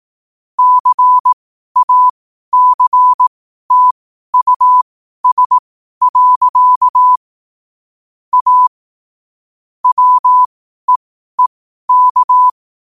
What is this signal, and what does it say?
Morse code, heard as a keyed tone, about 18 wpm, spaced as for 12 wpm, 1000 Hz. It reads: CACTUS. A WEEK
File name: qotdfiles / 12wpm